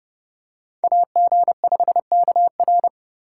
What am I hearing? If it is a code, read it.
Morse code, heard as a keyed tone, about 28 wpm, 700 Hz. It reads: AG5KR